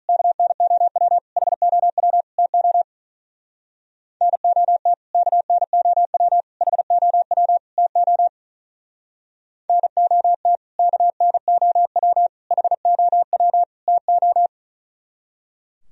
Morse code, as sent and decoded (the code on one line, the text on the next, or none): -.- -. --- .-- / .... --- .-- / - --- / -. --- - / -.- -. --- .-- / .... --- .-- / - --- / -. --- - / -.- -. --- .-- / .... --- .-- / - ---
KNOW HOW TO NOT KNOW HOW TO NOT KNOW HOW TO